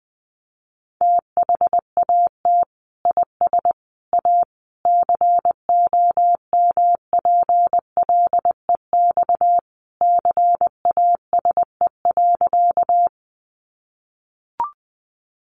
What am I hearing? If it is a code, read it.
Morse code, heard as a keyed tone, about 20 wpm, 700 Hz. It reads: THAT IS A COMPLEX CASE.
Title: Sentences from Top 900 Words 20wpm